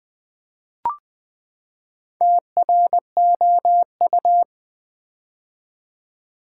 Morse code, heard as a keyed tone, about 20 wpm, 700 Hz. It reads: TROU
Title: Build A Word - 4 Letters At A Time - 8-Letter Words - 20wpm